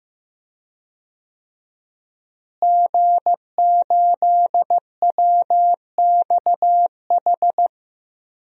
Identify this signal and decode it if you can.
Morse code, heard as a keyed tone, about 15 wpm, 700 Hz. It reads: G8WXH
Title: DXCC-100 Call Signs - 5 Characters - Repeated 2x - 15wpm